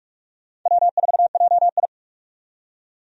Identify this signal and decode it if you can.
Morse code, heard as a keyed tone, about 46 wpm, 700 Hz. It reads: W4JI
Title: USA Call Signs - 4 Characters - 45wpm